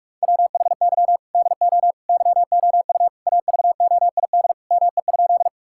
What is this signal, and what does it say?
Morse code, heard as a keyed tone, about 42 wpm, 700 Hz. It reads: WHY DO YOU AVOID ME?
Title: Sentences from Top 1000 Words - Review Part 6 - 45wpm